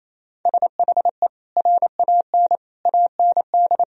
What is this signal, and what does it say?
Morse code, heard as a keyed tone, about 28 wpm, 700 Hz. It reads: SHE RAN AND
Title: Sentences from Top 1000 Words - Review Part 3 - 28wpm